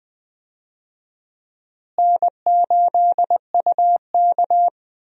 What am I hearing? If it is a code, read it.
Morse code, heard as a keyed tone, about 21 wpm, 700 Hz. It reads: N8UK
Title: USA Call Signs - 4 Characters - Repeated 3x - 20wpm